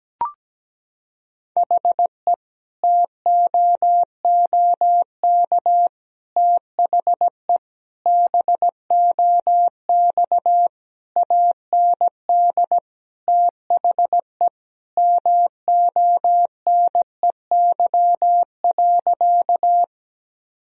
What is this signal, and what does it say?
Morse code, heard as a keyed tone, about 17 wpm, 700 Hz. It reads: HE TOOK THE BOX AND THE MONEY.
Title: Sentences from Top 1000 Words - Review Part 2 - 17wpm